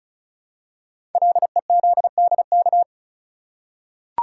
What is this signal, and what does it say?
Morse code, heard as a keyed tone, about 36 wpm, 690 Hz. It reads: LEZDK
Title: Code Groups - 5 Letters - 35wpm